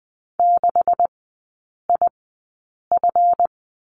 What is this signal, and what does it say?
Morse code, heard as a keyed tone, about 20 wpm, spaced as for 8 wpm, 700 Hz. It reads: 6IF
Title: 138 Lesson 23e - Letter f - Three Characters 20wpm